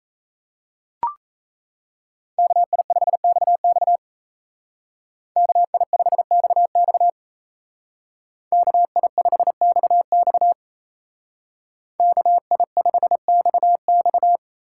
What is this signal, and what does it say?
Morse code, heard as a keyed tone, about 34 wpm, 700 Hz. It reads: KI5XX KI5XX KI5XX KI5XX
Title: Speed Racing - USA 2x2 Call Signs 28wpm